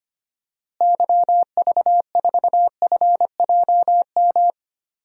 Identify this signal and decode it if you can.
Morse code, heard as a keyed tone, about 25 wpm, 700 Hz. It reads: YV4FJM
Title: DXCC-100 Call Signs - 6 Characters - Repeated 2x - 25wpm